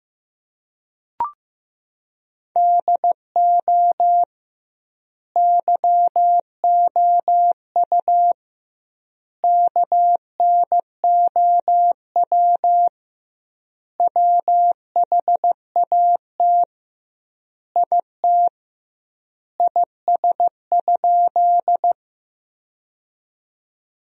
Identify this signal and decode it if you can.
Morse code, heard as a keyed tone, about 15 wpm, 700 Hz. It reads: DO YOU KNOW WHAT IT IS?
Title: 2x Word-Spacing - Sentences from Top 100 Words - 15wpm